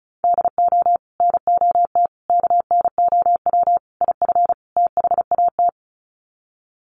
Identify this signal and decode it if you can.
Morse code, heard as a keyed tone, about 34 wpm, 700 Hz. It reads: DO NOT KNOW IF THAT
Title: Sets of 5 Words Encore2 35wpm